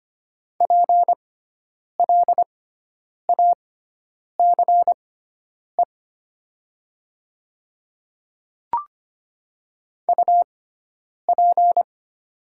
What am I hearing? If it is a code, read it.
Morse code, heard as a keyed tone, about 25 wpm, spaced as for 9 wpm, 700 Hz. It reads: PLACE UP
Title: 6x Character Spacing - Top 200 Words - 25wpm